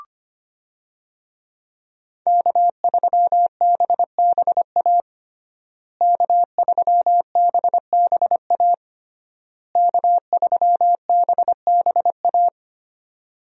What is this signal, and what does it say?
Morse code, heard as a keyed tone, about 25 wpm, 700 Hz. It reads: K3BBA K3BBA K3BBA
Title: USA Call Signs - 5 Characters - Repeated 3x - 25wpm